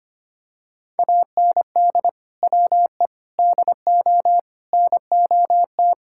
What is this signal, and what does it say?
Morse code, heard as a keyed tone, about 25 wpm, 700 Hz. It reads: AND WE DO NOT
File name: practice / 25wpm / Sentences with 4-Word Phrases - 25wpm